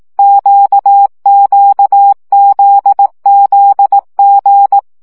Morse code, heard as a keyed tone, about 18 wpm, 800 Hz. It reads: QQZZG
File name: CW gyakorlas g z q 15 baud